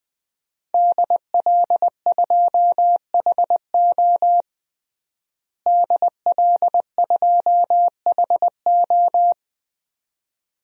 Morse code, heard as a keyed tone, about 20 wpm, 700 Hz. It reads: DL2HO DL2HO
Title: DXCC-100 Call Signs - 5 Characters - Repeated 2x - 20wpm